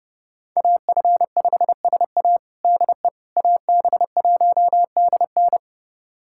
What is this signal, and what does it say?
Morse code, heard as a keyed tone, about 30 wpm, 700 Hz. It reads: AF5SA DE AB1DN
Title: USA Callsign DE Callsign - 5 Characters - 30wpm